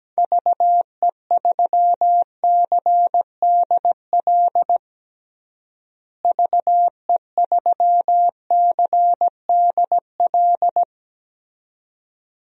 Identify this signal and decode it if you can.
Morse code, heard as a keyed tone, about 17 wpm, 700 Hz. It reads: VE3CDL VE3CDL
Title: DXCC-100 Call Signs - 6 Characters - Repeated 2x - 17wpm